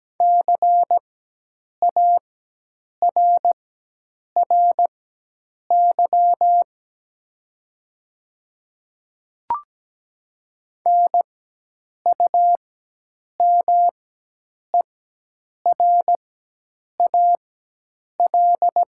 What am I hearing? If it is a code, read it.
Morse code, heard as a keyed tone, about 17 wpm, spaced as for 8 wpm, 700 Hz. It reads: CARRY NUMERAL
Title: 4x Character Spacing - Top 500 Words - 17wpm